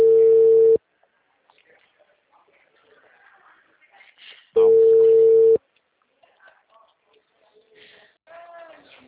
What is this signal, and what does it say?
Ringback tone (China, 450 Hz), with speech.